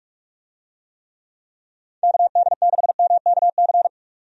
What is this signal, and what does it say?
Morse code, heard as a keyed tone, about 46 wpm, 700 Hz. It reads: KD6MKC
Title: USA Call Signs - 6 Characters - 45wpm